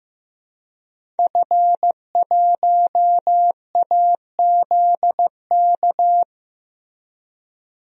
Morse code, heard as a keyed tone, about 15 wpm, 700 Hz. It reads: F1AZK
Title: DXCC-100 Call Signs - 5 Characters - Repeated 2x - 15wpm